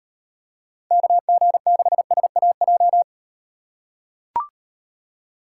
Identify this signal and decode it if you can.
Morse code, heard as a keyed tone, about 37 wpm, 700 Hz. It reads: KG6SAJ